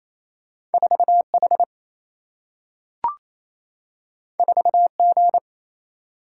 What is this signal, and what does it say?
Morse code, heard as a keyed tone, about 28 wpm, 700 Hz. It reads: 4H 4G
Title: Single-Letter Single-Number 28wpm